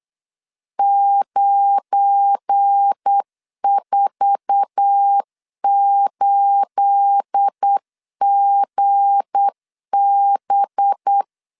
Morse code, heard as a keyed tone, about 8 wpm, 800 Hz. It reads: 948GB
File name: LEZIONI / 10 SIGNS / 10C DOT 8wpm 3649